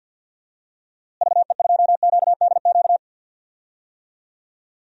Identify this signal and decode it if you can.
Morse code, heard as a keyed tone, about 47 wpm, 700 Hz. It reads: VE2QDQ
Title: DXCC-100 Call Signs - 6 Characters - Repeated 2x - 50wpm